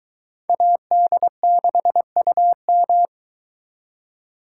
Morse code, heard as a keyed tone, about 22 wpm, 700 Hz. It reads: AD6UM